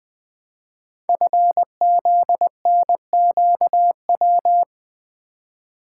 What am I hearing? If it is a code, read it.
Morse code, heard as a keyed tone, about 20 wpm, 700 Hz. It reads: FZNQW
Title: Code Groups - 5 Letters - 20wpm